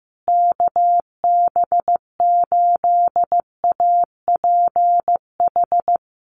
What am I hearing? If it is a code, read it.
Morse code, heard as a keyed tone, about 14 wpm, 700 Hz. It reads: KB8APH
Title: USA Call Signs Repeated 3x 15wpm